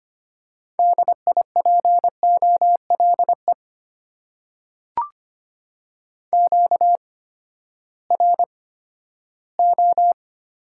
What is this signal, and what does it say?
Morse code, heard as a keyed tone, about 25 wpm, spaced as for 14 wpm, 700 Hz. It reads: DIPOLE QRO
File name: Top 90 QSO Elements - 8x Character Spacing - 25wpm